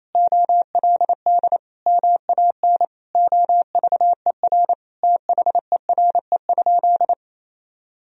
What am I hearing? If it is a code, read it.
Morse code, heard as a keyed tone, about 27 wpm, 700 Hz. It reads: OLD MAN OVER THERE?
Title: Sentences from Top 200 Words and Abbreviations 28wpm